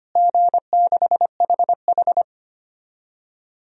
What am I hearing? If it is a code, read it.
Morse code, heard as a keyed tone, about 25 wpm, 700 Hz. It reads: G6HH
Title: DXCC-100 Call Signs - 4 Characters - Repeated 3x - 25wpm